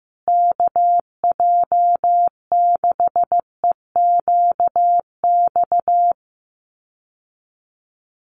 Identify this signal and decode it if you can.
Morse code, heard as a keyed tone, about 15 wpm, 700 Hz. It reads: KJ6EQX
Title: USA Call Signs Repeated 2x 15wpm